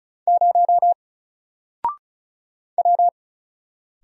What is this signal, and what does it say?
Morse code, heard as a keyed tone, about 36 wpm, 700 Hz. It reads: 0 W
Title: Single Character (Letter:Number:Punctuation:Prosign) 35wpm